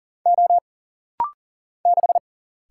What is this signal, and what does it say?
Morse code, heard as a keyed tone, about 40 wpm, 700 Hz. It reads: O 6
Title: Single Letter-Number Alt - Rapid-Fire 40wpm